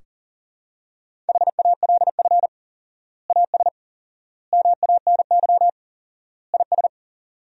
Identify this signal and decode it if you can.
Morse code, heard as a keyed tone, about 37 wpm, 700 Hz. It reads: HALF AS MANY IS
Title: 4x Word-Spacing - Sentences from Top 400 Words - 40wpm